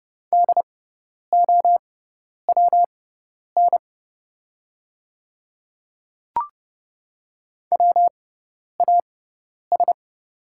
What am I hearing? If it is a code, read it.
Morse code, heard as a keyed tone, about 30 wpm, spaced as for 10 wpm, 700 Hz. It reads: DOWN WAS
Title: 6x Character Spacing - Top 200 Words - 30wpm